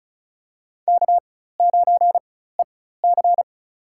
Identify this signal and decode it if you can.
Morse code, heard as a keyed tone, about 36 wpm, spaced as for 16 wpm, 700 Hz. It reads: K9EC
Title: USA Call Signs - 4 Characters - 4x Character Spacing - 35wpm